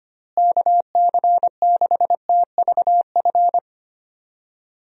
Morse code, heard as a keyed tone, about 25 wpm, 700 Hz. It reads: KC6TVF
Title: USA Call Signs - 6 Characters - Repeated 2x - 25wpm